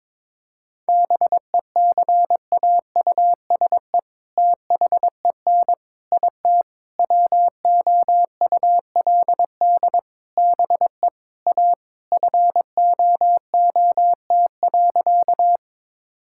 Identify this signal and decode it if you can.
Morse code, heard as a keyed tone, about 22 wpm, 700 Hz. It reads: BECAUSE THEN IT WOULD BE A FOOT.
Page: morse-code-ninja-practice-22wpm